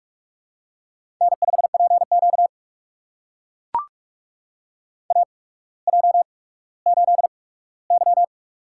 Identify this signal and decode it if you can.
Morse code, heard as a keyed tone, about 44 wpm, spaced as for 19 wpm, 700 Hz. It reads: N5PQ AJ8Y
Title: USA Call Signs - 4 Characters - 8x Character Spacing - 45wpm